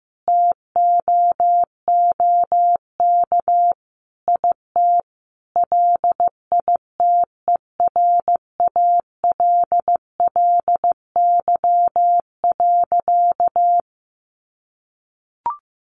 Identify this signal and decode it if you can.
Morse code, heard as a keyed tone, about 15 wpm, 700 Hz. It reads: TOOK IT LITERALLY.
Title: Sentences from Top 1400 Words 15wpm